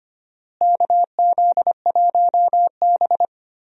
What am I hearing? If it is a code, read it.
Morse code, heard as a keyed tone, about 25 wpm, 700 Hz. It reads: KZ1B
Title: USA Call Signs - 4 Characters - Repeated 3x - 25wpm